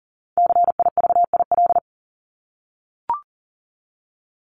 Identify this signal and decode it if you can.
Morse code, heard as a keyed tone, about 40 wpm, 700 Hz. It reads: CIVIL
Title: Top 1751-2000 Words 40wpm